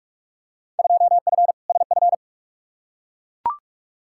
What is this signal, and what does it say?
Morse code, heard as a keyed tone, about 42 wpm, 700 Hz. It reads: 2F SF
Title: Field Day Exchanges - 45wpm